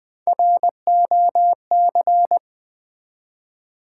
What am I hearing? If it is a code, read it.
Morse code, heard as a keyed tone, about 21 wpm, 700 Hz. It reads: ROC
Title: Build A Word - 1 Letter At A Time - 4-Letter Words - 20wpm